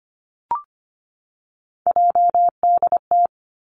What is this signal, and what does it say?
Morse code, heard as a keyed tone, about 25 wpm, 700 Hz. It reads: JDT
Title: Three Challenging Letter-Number Combinations 25wpm